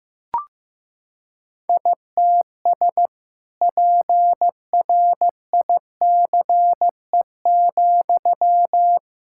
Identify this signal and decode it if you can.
Morse code, heard as a keyed tone, about 15 wpm, 700 Hz. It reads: ITS PRICE,
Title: Sentences from Top 1000 Words - Review Part 5 - 15wpm